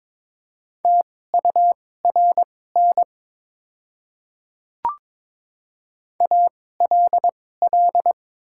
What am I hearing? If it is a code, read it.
Morse code, heard as a keyed tone, about 22 wpm, spaced as for 16 wpm, 700 Hz. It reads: TURN ALL
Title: 2x Character Spacing - Top 200 Words - 22wpm